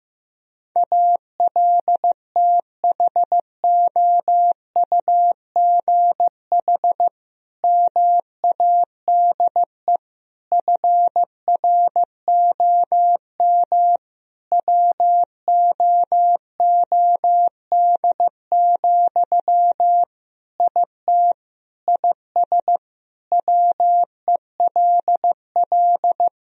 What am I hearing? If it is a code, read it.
Morse code, heard as a keyed tone, about 16 wpm, 700 Hz. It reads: ALTHOUGH MADE FROM WOOD, IT IS WELL
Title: Sentences from Top 1000 Words - Review Part 3 - 15wpm